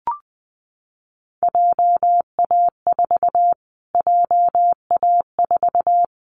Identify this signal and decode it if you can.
Morse code, heard as a keyed tone, about 19 wpm, 700 Hz. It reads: JA4 JA4
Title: Callsign Prefixes for Top 25 DXCC - Repeated 3x - 20wpm